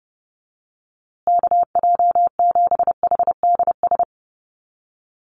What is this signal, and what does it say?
Morse code, heard as a keyed tone, about 30 wpm, 700 Hz. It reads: KJ7HDS
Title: USA Call Signs Repeated 3x 30wpm